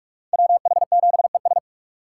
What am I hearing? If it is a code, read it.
Morse code, heard as a keyed tone, about 46 wpm, 700 Hz. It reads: WH7ES